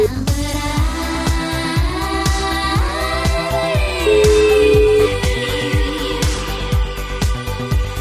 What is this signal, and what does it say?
Ringback tone (Europe, 425 Hz), with noise.